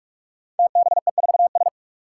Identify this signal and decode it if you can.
Morse code, heard as a keyed tone, about 44 wpm, 700 Hz. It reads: TBE4S
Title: Five Challenging Letter-Number Combinations 45wpm